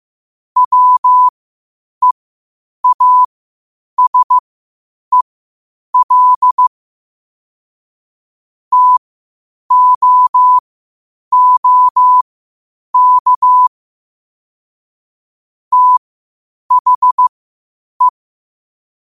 Morse code, heard as a keyed tone, about 15 wpm, spaced as for 8 wpm, 1000 Hz. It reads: WEASEL TOOK THE
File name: qotdfiles / 8wpm